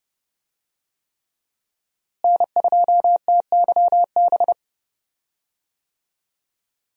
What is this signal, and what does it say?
Morse code, heard as a keyed tone, about 30 wpm, 700 Hz. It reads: N2TYB
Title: USA Call Signs - 5 Characters - Repeated 3x - 30wpm